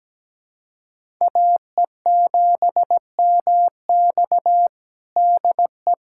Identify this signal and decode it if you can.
Morse code, heard as a keyed tone, about 17 wpm, 700 Hz. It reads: AE7MX DE
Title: USA Callsign DE Callsign 17wpm